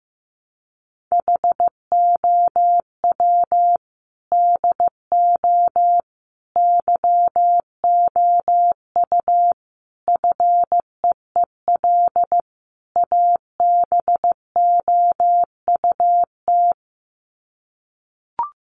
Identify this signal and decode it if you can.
Morse code, heard as a keyed tone, about 15 wpm, 700 Hz. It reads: HOW DO YOU FEEL ABOUT
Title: Sets of 5 Words 15wpm